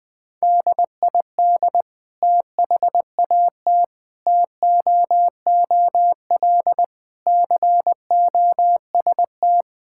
Morse code, heard as a keyed tone, about 21 wpm, 700 Hz. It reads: DID THAT TOOL COST
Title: Sentences from Top 1000 Words - Review Part 6 - 20wpm